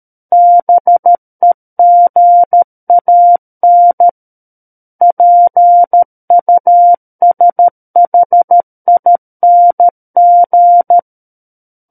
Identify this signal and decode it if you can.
Morse code, heard as a keyed tone, about 13 wpm, 700 Hz. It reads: BEGAN PUSHING
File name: Book One / 04-Chapter 4 - 13 WPM